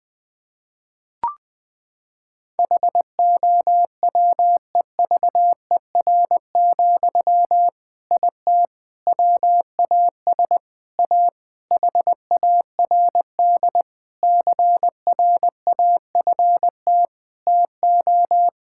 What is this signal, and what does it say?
Morse code, heard as a keyed tone, about 20 wpm, 700 Hz. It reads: HOWEVER, IT WAS A HARD CRAFT TO
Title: Sentences from Top 1000 Words - Review Part 6 - 20wpm